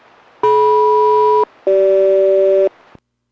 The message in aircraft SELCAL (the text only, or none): DM-CG